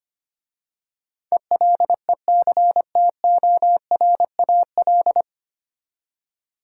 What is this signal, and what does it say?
Morse code, heard as a keyed tone, about 25 wpm, 690 Hz. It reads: ELECTORAL
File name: practice / 25wpm / 9 Letter Words - First 3 Letters Given - 25wpm